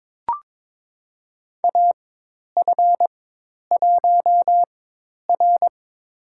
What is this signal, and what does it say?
Morse code, heard as a keyed tone, about 22 wpm, spaced as for 10 wpm, 700 Hz. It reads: AF1R
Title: USA Call Signs - 4x Character Spacing - 22wpm